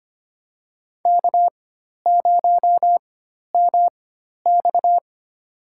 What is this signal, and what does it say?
Morse code, heard as a keyed tone, about 25 wpm, spaced as for 12 wpm, 700 Hz. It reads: K0MX
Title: USA Call Signs - 4x Character Spacing - 25wpm